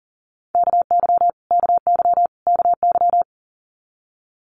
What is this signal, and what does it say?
Morse code, heard as a keyed tone, about 42 wpm, 700 Hz. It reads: KY KY KY
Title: Multi-letter Phonemes Repeated 3x 40wpm